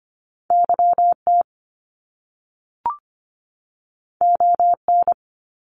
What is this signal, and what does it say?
Morse code, heard as a keyed tone, about 25 wpm, 700 Hz. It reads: YT ON
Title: Canadian Provinces and Territories 25wpm